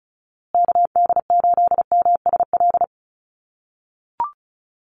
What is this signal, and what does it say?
Morse code, heard as a keyed tone, about 34 wpm, 700 Hz. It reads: KD8MSL
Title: USA Call Signs Repeated 3x 35wpm